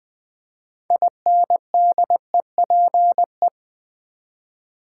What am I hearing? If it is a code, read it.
Morse code, heard as a keyed tone, about 20 wpm, 700 Hz. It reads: INDEPE
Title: Build A Word - 6 Letters At A Time - 12-Letter Words - 20wpm